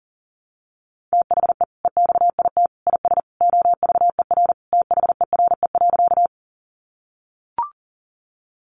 Morse code, heard as a keyed tone, about 40 wpm, 700 Hz. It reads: THE EXIT IS OVER THERE.
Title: Sentences from Top 1100 Words 40wpm